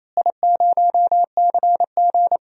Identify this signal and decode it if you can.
Morse code, heard as a keyed tone, about 28 wpm, 700 Hz. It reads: I0CG
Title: DXCC-100 Call Signs - 4 Characters - Repeated 2x - 28wpm